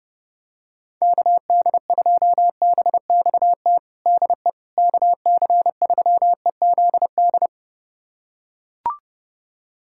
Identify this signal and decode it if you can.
Morse code, heard as a keyed tone, about 30 wpm, 700 Hz. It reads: KD2BXT DE KC3EZD
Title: USA Callsign DE Callsign 30wpm